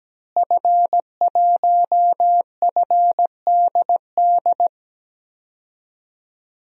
Morse code, heard as a keyed tone, about 17 wpm, 700 Hz. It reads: F1FDD
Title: DXCC-100 Call Signs - 5 Characters - Repeated 2x - 17wpm